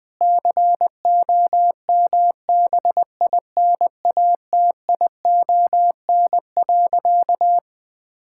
Morse code, heard as a keyed tone, about 21 wpm, 700 Hz. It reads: COMBINATION.
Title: Sentences from Top 1000 Words - Review Part 6 - 20wpm